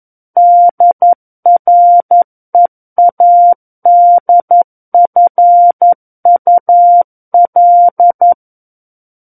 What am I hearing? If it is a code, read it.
Morse code, heard as a keyed tone, about 11 wpm, 700 Hz. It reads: DREADFUL
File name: 11 WPM